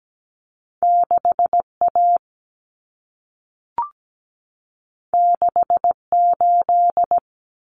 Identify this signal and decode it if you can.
Morse code, heard as a keyed tone, about 17 wpm, 700 Hz. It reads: 6A 68
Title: Common Cut Numbers - 2-Digit 17wpm